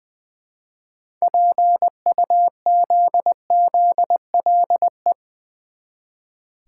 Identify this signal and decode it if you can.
Morse code, heard as a keyed tone, about 20 wpm, 700 Hz. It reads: PUZZLE
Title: Easy Words with Hints - 20wpm